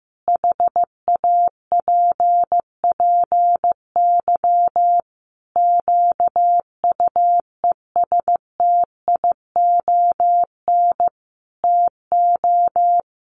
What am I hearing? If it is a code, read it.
Morse code, heard as a keyed tone, about 15 wpm, 700 Hz. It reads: HAPPY QUESTION TO
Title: Sentences from Top 1000 Words 15wpm